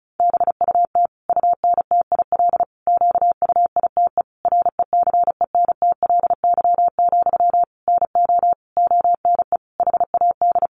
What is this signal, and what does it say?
Morse code, heard as a keyed tone, about 34 wpm, 700 Hz. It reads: BUT UNTIL QUITE RECENTLY, NO ONE HAD
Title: Sentences from Top 1200 Words 35wpm